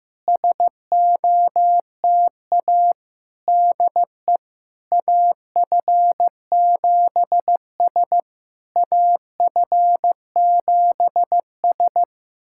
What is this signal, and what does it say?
Morse code, heard as a keyed tone, about 15 wpm, 700 Hz. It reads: SOTA DE AF7S AF7S
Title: SOTA QSO Practice 15wpm